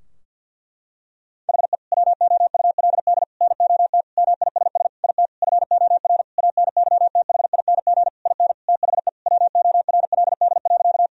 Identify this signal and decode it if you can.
Morse code, heard as a keyed tone, about 50 wpm, 700 Hz. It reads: HE WOULD NOT MISS IT FOR ANYTHING IN THE WORLD.
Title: Sentences from Top 1000 Words - Review Part 3 - 50wpm